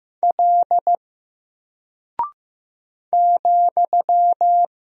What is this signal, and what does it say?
Morse code, heard as a keyed tone, about 15 wpm, 700 Hz. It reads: L ,
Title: Single Character (Letter:Number:Punctuation:Prosign) 15wpm